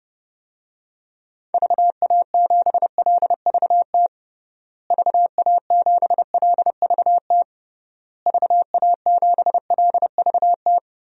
Morse code, heard as a keyed tone, about 30 wpm, 700 Hz. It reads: VA7LVT VA7LVT VA7LVT
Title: DXCC-100 Call Signs - 6 Characters - Repeated 3x - 30wpm